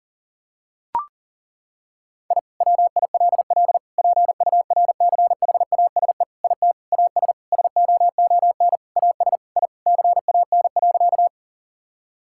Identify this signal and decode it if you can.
Morse code, heard as a keyed tone, about 37 wpm, 700 Hz. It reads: I WILL PURCHASE IT AS SOON AS I CAN.
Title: Sentences from Top 1000 Words - Review Part 5 - 40wpm